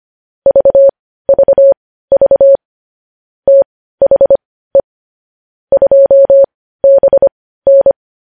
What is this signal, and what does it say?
Morse code, heard as a keyed tone, about 25 wpm, spaced as for 15 wpm, 550 Hz. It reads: VVV THE 2BN
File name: podcast / News headlines-1x-25x15